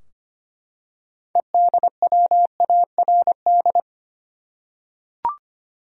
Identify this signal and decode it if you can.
Morse code, heard as a keyed tone, about 25 wpm, 700 Hz. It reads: EDWARD
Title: USA Common Names - 25wpm